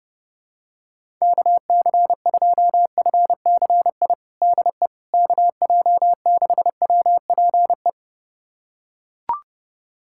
Morse code, heard as a keyed tone, about 29 wpm, 700 Hz. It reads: KC2FCI DE KJ6WPE
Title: USA Callsign DE Callsign 30wpm